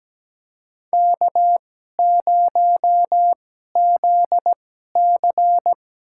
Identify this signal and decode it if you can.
Morse code, heard as a keyed tone, about 17 wpm, spaced as for 12 wpm, 700 Hz. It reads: K0ZC
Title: USA Call Signs - 4 Characters - 2x Character Spacing - 17wpm